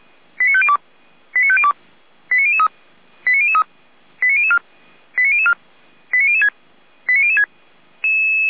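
Five-tone selective call (ZVEI), heard as quick runs of five tones, tones of about 80 ms. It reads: 89502 89502 89003 89003 89004 89004 89006 89006
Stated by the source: ZVEI five-tone selective call 89010, received on SDR; audio hosted by feuerwehr-orlamuende (fire brigade Orlamünde).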